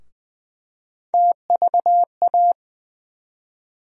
Morse code, heard as a keyed tone, about 20 wpm, 700 Hz. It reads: TVA
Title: USA Call Sign Postfixes - 3 Letters 20wpm